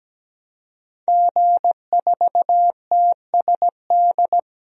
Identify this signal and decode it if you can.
Morse code, heard as a keyed tone, about 17 wpm, 700 Hz. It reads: G4TSD
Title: DXCC-100 Call Signs - 5 Characters - 17wpm